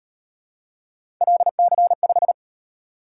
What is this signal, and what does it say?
Morse code, heard as a keyed tone, about 38 wpm, 700 Hz. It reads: LC5